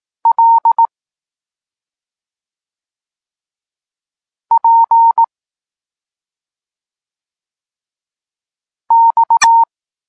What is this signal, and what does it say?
Morse code, heard as a keyed tone, about 18 wpm, 920 Hz. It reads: L P X
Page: Tekst 32 – Doskonalenie odbioru poznanych liter